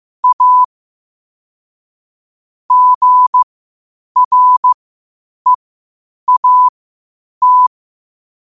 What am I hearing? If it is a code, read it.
Morse code, heard as a keyed tone, about 15 wpm, spaced as for 8 wpm, 1000 Hz. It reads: A GREAT
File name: qotdfiles / 8wpm